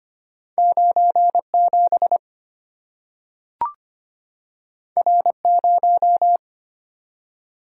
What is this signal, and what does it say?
Morse code, heard as a keyed tone, about 25 wpm, 700 Hz. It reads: 97 R0
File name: Two Letter-Number Combinations - 25wpm